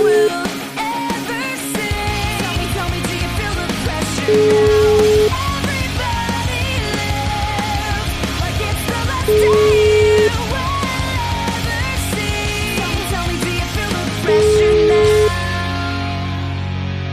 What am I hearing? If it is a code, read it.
Ringback tone (Europe, 425 Hz), with noise.